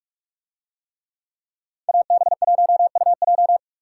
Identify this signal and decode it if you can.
Morse code, heard as a keyed tone, about 42 wpm, 700 Hz. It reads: AB1UJ